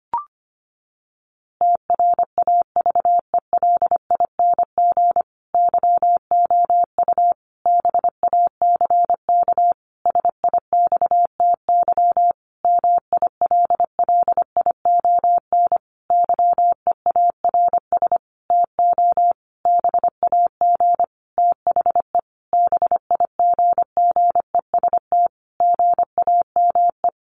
Morse code, heard as a keyed tone, about 25 wpm, 700 Hz. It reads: TRAVELING YOU BACK SIXTY MILLION YEARS TO BAG THE BIGGEST GAME
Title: A Sound Of Thunder 25wpm